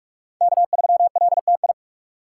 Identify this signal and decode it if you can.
Morse code, heard as a keyed tone, about 41 wpm, 700 Hz. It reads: K3LTI
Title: USA Call Signs - 5 Characters - 2x Character Spacing - 45wpm